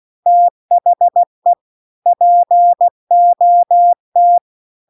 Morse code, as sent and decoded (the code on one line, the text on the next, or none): - .... / .--. --- -
THE POT